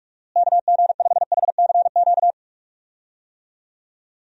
Morse code, heard as a keyed tone, about 43 wpm, 700 Hz. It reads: KG5HCQ
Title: USA Call Signs - 6 Characters - Repeated 2x - 45wpm